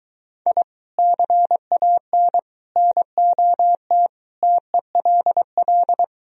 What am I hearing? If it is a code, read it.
Morse code, heard as a keyed tone, about 23 wpm, 700 Hz. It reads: I CAN NOT TELL